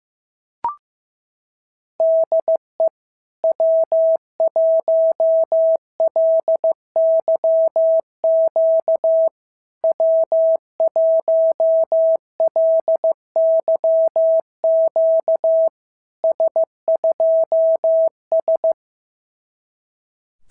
Morse code, heard as a keyed tone, about 14 wpm, 650 Hz. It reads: DE W1LYQ W1LYQ S2S
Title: SOTA QSO Practice 15wpm